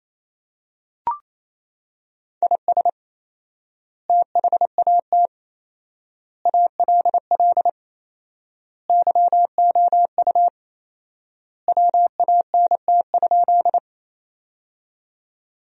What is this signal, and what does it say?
Morse code, heard as a keyed tone, about 28 wpm, 700 Hz. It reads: IS THAT ALL YOU WANT?
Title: 4x Word-Spacing - Sentences from Top 200 Words - 28wpm